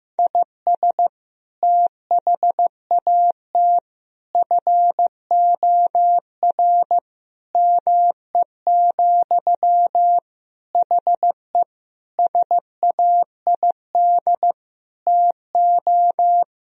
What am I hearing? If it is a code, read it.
Morse code, heard as a keyed tone, about 15 wpm, 700 Hz. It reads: IS THAT FOR ME, HE SAID TO
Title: Sentences from Top 1000 Words - Review Part 1 - 15wpm